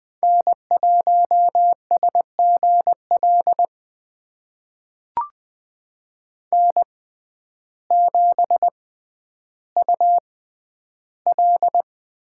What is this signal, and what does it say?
Morse code, heard as a keyed tone, about 20 wpm, spaced as for 11 wpm, 700 Hz. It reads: N1SGL N7UL
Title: USA Call Signs - 5 Characters - 6x Character Spacing - 20wpm